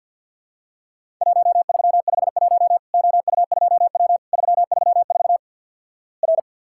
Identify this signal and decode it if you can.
Morse code, heard as a keyed tone, about 50 wpm, 700 Hz. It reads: 1351 OU2W 334 R
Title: Worked All Europe DX Contest - QTC Exchange with Repeats - 50wpm